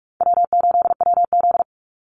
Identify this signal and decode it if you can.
Morse code, heard as a keyed tone, about 44 wpm, 700 Hz. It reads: W8WZ
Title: Call Signs - 45wpm